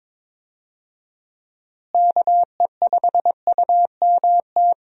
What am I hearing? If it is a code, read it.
Morse code, heard as a keyed tone, about 22 wpm, 700 Hz. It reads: KE5UMT
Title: USA Call Signs - 6 Characters - Repeated 2x - 22wpm